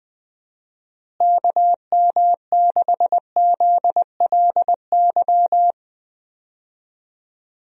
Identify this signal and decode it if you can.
Morse code, heard as a keyed tone, about 21 wpm, 700 Hz. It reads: KM6ZLY
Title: USA Call Signs - 6 Characters - Repeated 2x - 20wpm